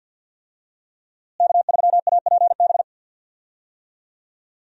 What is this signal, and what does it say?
Morse code, heard as a keyed tone, about 51 wpm, 700 Hz. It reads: K3APB
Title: USA Call Signs - 5 Characters - Repeated 2x - 50wpm